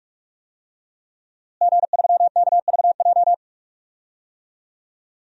Morse code, heard as a keyed tone, about 41 wpm, 700 Hz. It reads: G3KVJ
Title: DXCC-100 Call Signs - 5 Characters - Repeated 2x - 45wpm